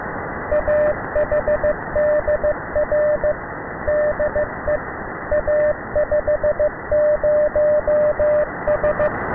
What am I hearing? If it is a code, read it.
Morse code, heard as a keyed tone, about 15 wpm, 600 Hz, with radio noise. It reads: AHDR DE A50S